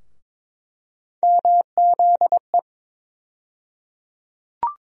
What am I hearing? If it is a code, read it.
Morse code, heard as a keyed tone, about 22 wpm, 700 Hz. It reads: MZE
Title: USA Call Sign Postfixes - 3 Letters 22wpm